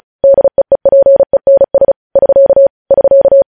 Morse code, heard as a keyed tone, about 36 wpm, 550 Hz. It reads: DEEPENS <SK> <SK>
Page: News Headlines in Morse Code at 35 WPM